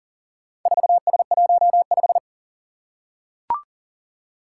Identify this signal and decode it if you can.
Morse code, heard as a keyed tone, about 40 wpm, 700 Hz. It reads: 4S15
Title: Four Challenging Letter-Number Combinations 40wpm